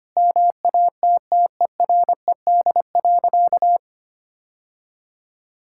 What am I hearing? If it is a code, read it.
Morse code, heard as a keyed tone, about 25 wpm, 700 Hz. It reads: MATTERED.
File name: practice / 25wpm / Sentences with 4-Word Phrases - 25wpm